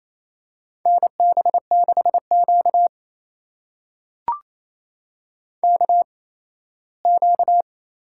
Morse code, heard as a keyed tone, about 27 wpm, spaced as for 17 wpm, 700 Hz. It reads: NB6Q KQ
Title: USA Call Signs - 4 Characters - 8x Character Spacing - 28wpm